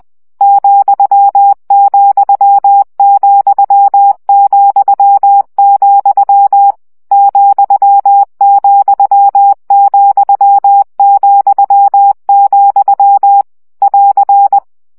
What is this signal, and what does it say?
Morse code, heard as a keyed tone, about 20 wpm, 800 Hz. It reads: ,,,,, ,,,,, <AR>
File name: CW gyakorlas vesszo 17 baud